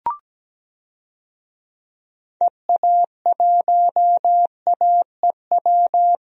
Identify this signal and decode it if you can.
Morse code, heard as a keyed tone, about 17 wpm, 700 Hz. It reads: EA1AEW
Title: DXCC-100 Call Signs - 6 Characters - Repeated 3x - 17wpm